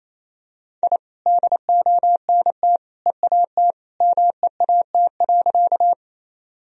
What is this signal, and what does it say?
Morse code, heard as a keyed tone, about 28 wpm, 700 Hz. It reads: I DONT EAT MEAT.
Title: Sentences from Top 1000 Words - Review Part 3 - 28wpm